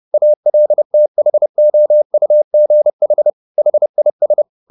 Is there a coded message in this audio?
Yes: Morse code from 0.1 to 4.4 s, heard as a keyed tone, about 30 wpm, 600 Hz. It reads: ALTHOUGH HIS